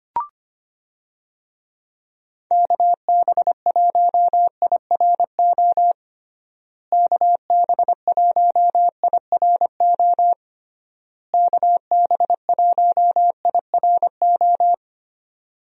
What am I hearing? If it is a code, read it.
Morse code, heard as a keyed tone, about 25 wpm, 700 Hz. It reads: KB1IRO KB1IRO KB1IRO
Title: USA Call Signs - 6 Characters - Repeated 3x - 25wpm